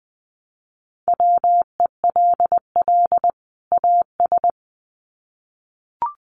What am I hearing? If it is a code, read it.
Morse code, heard as a keyed tone, about 20 wpm, 700 Hz. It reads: WELL AS
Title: Sets of 2 Words 20wpm